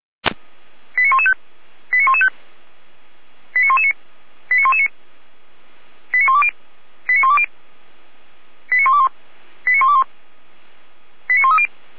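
Five-tone selective call (ZVEI), heard as quick runs of five tones, tones of about 70 ms. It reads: 89115 89115 89119 89119 89120 89120 89121 89121 89130
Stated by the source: Five-tone ZVEI selective call 89097 received on an SDR.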